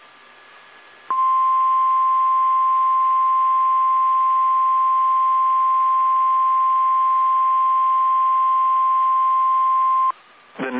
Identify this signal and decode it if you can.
Emergency Alert System (SAME) tones (the 1050 Hz attention tone). It is heard from 1.1 to 10.1 s.
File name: eas / KSDZKDJL / KSDZKDJL